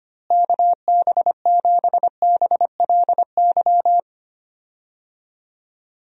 Morse code, heard as a keyed tone, about 25 wpm, 700 Hz. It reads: KB7BLY
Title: USA Call Signs - 6 Characters - Repeated 3x - 25wpm